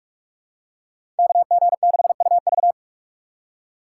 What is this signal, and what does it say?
Morse code, heard as a keyed tone, about 44 wpm, 700 Hz. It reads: KG6UV
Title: USA Call Signs - 5 Characters - 3x Character Spacing - 45wpm